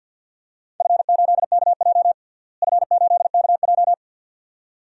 Morse code, heard as a keyed tone, about 46 wpm, 700 Hz. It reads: F8KJ F8KJ
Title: DXCC-100 Call Signs - 4 Characters - Repeated 2x - 50wpm